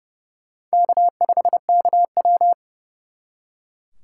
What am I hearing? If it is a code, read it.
Morse code, heard as a keyed tone, about 38 wpm, 700 Hz. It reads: K5KW